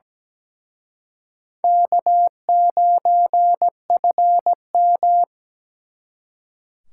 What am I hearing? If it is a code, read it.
Morse code, heard as a keyed tone, about 20 wpm, 700 Hz. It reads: K9FM
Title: Speed Racing - USA 1x2 Call Signs 17wpm